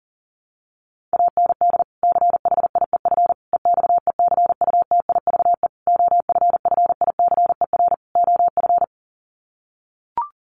Morse code, heard as a keyed tone, about 40 wpm, 700 Hz. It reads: AND CHIEF EXECUTIVE OFFICER OF
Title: Sets of 5 Words Encore2 40wpm